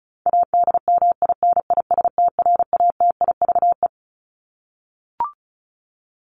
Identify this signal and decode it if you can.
Morse code, heard as a keyed tone, about 34 wpm, 700 Hz. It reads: ADMINISTRATIVE
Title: Top 1501-1750 Words 35wpm